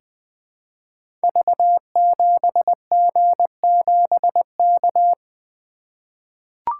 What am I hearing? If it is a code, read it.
Morse code, heard as a keyed tone, about 20 wpm, 700 Hz. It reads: V7G7K
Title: Code Groups - 5 Letter:Numbers - 20wpm